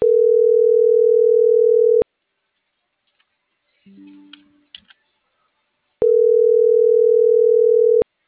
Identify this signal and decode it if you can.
Ringback tone (North America), 440 + 480 Hz.